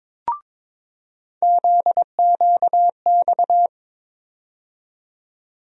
Morse code, heard as a keyed tone, about 22 wpm, 700 Hz. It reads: ZQX
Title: USA Call Sign Postfixes - 3 Letters 22wpm